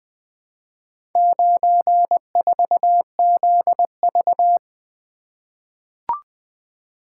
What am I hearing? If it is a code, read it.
Morse code, heard as a keyed tone, about 20 wpm, 700 Hz. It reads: 94ZV
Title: Four Letter-Number Combinations - 20wpm